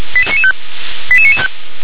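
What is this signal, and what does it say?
Five-tone selective call (ZVEI), heard as quick runs of five tones, tones of about 70 ms. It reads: 88995
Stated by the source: ZVEI five-tone call received on an SDR.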